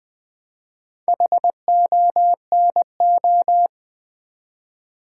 Morse code, heard as a keyed tone, about 20 wpm, 700 Hz. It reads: HONO
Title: Build A Word - 2 Letters At A Time - 6-Letter Words - 20wpm